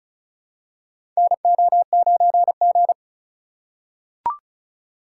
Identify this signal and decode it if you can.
Morse code, heard as a keyed tone, about 36 wpm, 700 Hz. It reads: NO9G
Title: USA Call Signs - 4 Characters - 4x Character Spacing - 35wpm